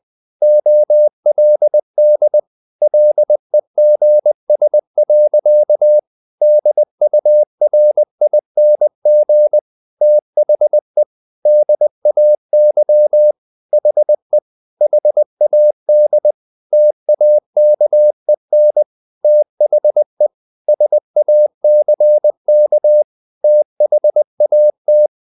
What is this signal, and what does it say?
Morse code, heard as a keyed tone, about 20 wpm, 600 Hz. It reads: OLD LEGS. DURING THE DAY HE HAD TAKEN THE SACK THAT
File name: the old man and the sea / the old man and the sea part 1 20wpm